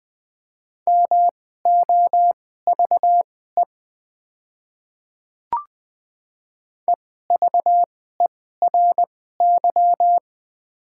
Morse code, heard as a keyed tone, about 20 wpm, spaced as for 15 wpm, 690 Hz. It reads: MOVE EVERY
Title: 2x Character Spacing - Top 500 Words - 20wpm